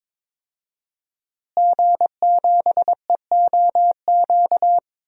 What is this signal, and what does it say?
Morse code, heard as a keyed tone, about 22 wpm, 700 Hz. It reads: G7EOQ
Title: DXCC-100 Call Signs - 5 Characters - Repeated 2x - 22wpm